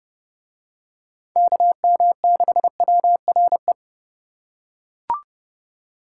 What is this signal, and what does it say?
Morse code, heard as a keyed tone, about 28 wpm, 700 Hz. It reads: KM6WRE